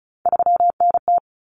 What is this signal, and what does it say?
Morse code, heard as a keyed tone, about 34 wpm, 700 Hz. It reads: 3NT
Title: Common Cut Numbers - 3-Digit 35wpm